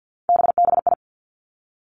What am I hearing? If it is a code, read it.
Morse code, heard as a keyed tone, about 50 wpm, 700 Hz. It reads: BBI
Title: Three Challenging Letter-Number Combinations 50wpm